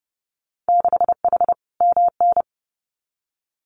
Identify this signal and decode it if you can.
Morse code, heard as a keyed tone, about 31 wpm, 700 Hz. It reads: 6H MN
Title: Winter Field Day Exchanges Encore 30wpm